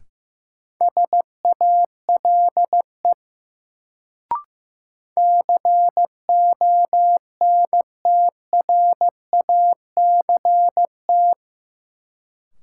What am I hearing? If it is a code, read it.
Morse code, heard as a keyed tone, about 16 wpm, 700 Hz. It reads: SALE CONTRACT
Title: Sentences from Top 1400 Words - New Word Drill - 15wpm